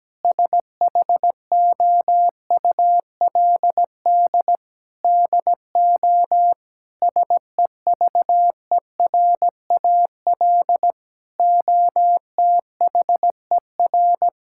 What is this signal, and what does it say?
Morse code, heard as a keyed tone, about 17 wpm, 700 Hz. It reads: SHOULD DO SEVERAL OTHER
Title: Sentences from Top 1000 Words - Review Part 6 - 17wpm